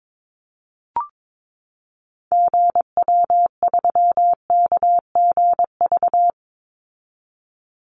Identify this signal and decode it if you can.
Morse code, heard as a keyed tone, about 22 wpm, 700 Hz. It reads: GW3KGV
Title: Call Signs Repeated 3x - 22wpm